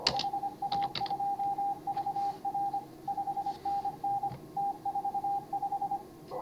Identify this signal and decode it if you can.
Morse code, heard as a keyed tone, about 26 wpm, 810 Hz. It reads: HA2VR 5NN T45
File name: CW / HA5MA RHOB HANG CW 2017